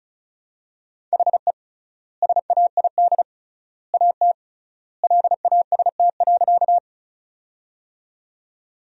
Morse code, heard as a keyed tone, about 36 wpm, 700 Hz. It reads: HE SAID AT LAST.
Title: 3x Word-Spacing - Sentences from Top 300 Words - 35wpm